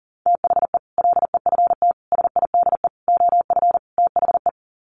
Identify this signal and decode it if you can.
Morse code, heard as a keyed tone, about 40 wpm, 700 Hz. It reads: THE LEFT SIDE OF THE
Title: Sets of 5 Words Encore 40wpm